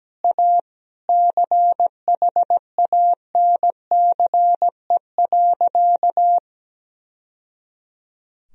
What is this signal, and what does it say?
Morse code, heard as a keyed tone, about 17 wpm, 700 Hz. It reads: A CHANCE.
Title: Sentences from Top 1000 Words - Review Part 2 - 17wpm